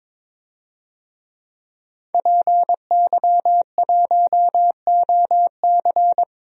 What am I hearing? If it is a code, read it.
Morse code, heard as a keyed tone, about 22 wpm, 700 Hz. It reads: PY1OC
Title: DXCC-100 Call Signs - 5 Characters - 22wpm